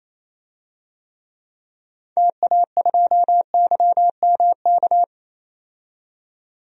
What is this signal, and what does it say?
Morse code, heard as a keyed tone, about 28 wpm, 700 Hz. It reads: TA2YMK
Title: DXCC-100 Call Signs - 6 Characters - 28wpm